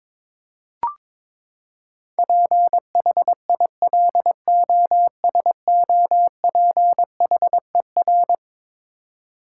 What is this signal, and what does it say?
Morse code, heard as a keyed tone, about 22 wpm, 700 Hz. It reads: PHILOSOPHER
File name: practice / 22wpm / Eleven Letter Words - 22wpm